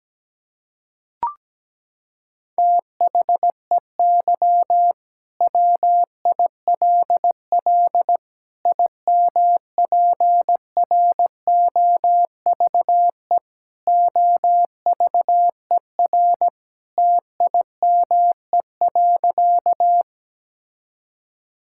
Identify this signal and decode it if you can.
Morse code, heard as a keyed tone, about 17 wpm, 700 Hz. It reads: THEY WILL IMPROVE OVER TIME.
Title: Sentences from Top 1000 Words - Review Part 5 - 17wpm